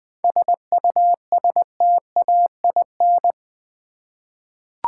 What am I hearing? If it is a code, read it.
Morse code, heard as a keyed tone, about 20 wpm, 700 Hz. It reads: SUSTAIN
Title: 7 Letter Words - First 3 Letters Given - 20wpm